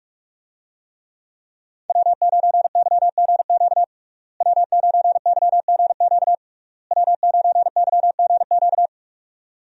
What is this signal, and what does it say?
Morse code, heard as a keyed tone, about 46 wpm, 700 Hz. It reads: W9YGQ W9YGQ W9YGQ
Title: USA Call Signs - 5 Characters - Repeated 3x - 45wpm